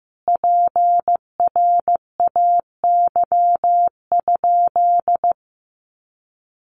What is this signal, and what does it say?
Morse code, heard as a keyed tone, about 15 wpm, 700 Hz. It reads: PRAY?
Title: A Sound Of Thunder 15wpm